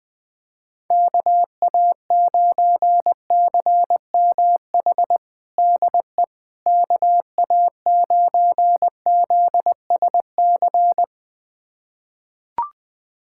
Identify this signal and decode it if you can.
Morse code, heard as a keyed tone, about 21 wpm, 700 Hz. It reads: KA9CMH DE KA9ZSC
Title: USA Callsign DE Callsign 20wpm